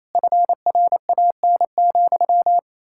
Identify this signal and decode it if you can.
Morse code, heard as a keyed tone, about 35 wpm, 700 Hz. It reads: FRAN,